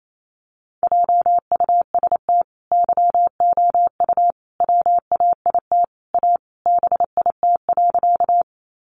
Morse code, heard as a keyed tone, about 28 wpm, 700 Hz. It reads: JUST YOU WAIT A BIT.
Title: Sentences from Top 900 Words 28wpm